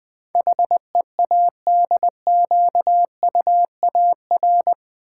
Morse code, heard as a keyed tone, about 20 wpm, 700 Hz. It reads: HEADQUAR
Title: Build A Word - 4 Letters At A Time - 12-Letter Words - 20wpm